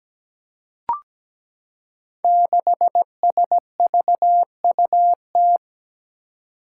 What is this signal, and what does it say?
Morse code, heard as a keyed tone, about 17 wpm, 700 Hz. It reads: 6SVUT
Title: Five Challenging Letter-Number Combinations 17wpm